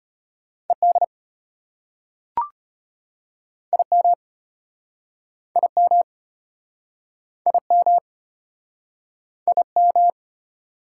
Morse code, heard as a keyed tone, about 33 wpm, 700 Hz. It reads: ED IM IM IM IM